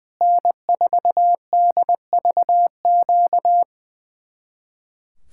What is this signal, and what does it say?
Morse code, heard as a keyed tone, about 21 wpm, 700 Hz. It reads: N4DVQ
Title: Speed-Racing Common QSO Phrases - 20wpm